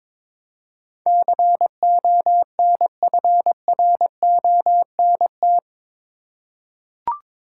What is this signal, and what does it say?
Morse code, heard as a keyed tone, about 22 wpm, 700 Hz. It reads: CONFRONT
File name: practice / 22wpm / Eight Letter Words - 22wpm